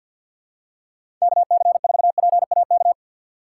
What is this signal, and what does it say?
Morse code, heard as a keyed tone, about 51 wpm, 700 Hz. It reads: KC4PAK
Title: USA Call Signs - 6 Characters - 4x Character Spacing - 50wpm